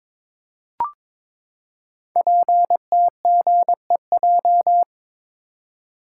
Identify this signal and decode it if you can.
Morse code, heard as a keyed tone, about 22 wpm, 700 Hz. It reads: PTGEJ
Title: Code Groups - 5 Letters - 22wpm